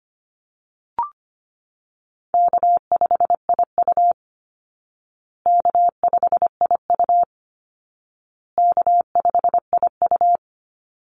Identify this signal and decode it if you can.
Morse code, heard as a keyed tone, about 25 wpm, 700 Hz. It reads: K5IU K5IU K5IU
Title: Easy Call Signs Repeated 3x - 25wpm